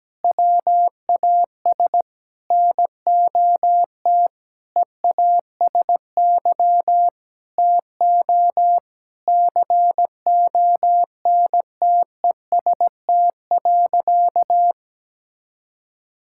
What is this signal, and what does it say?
Morse code, heard as a keyed tone, about 17 wpm, 700 Hz. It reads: WAS NOT EASY TO CONTEST.
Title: Sentences from Top 1600 Words 17wpm